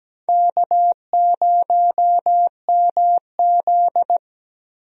Morse code, heard as a keyed tone, about 17 wpm, 700 Hz. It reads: K0MZ
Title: USA Call Signs - 4 Characters - Repeated 3x - 17wpm